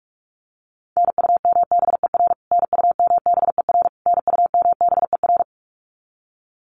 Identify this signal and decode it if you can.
Morse code, heard as a keyed tone, about 44 wpm, 700 Hz. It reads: NUMBER NUMBER NUMBER
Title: Top 200 Words Repeated 45wpm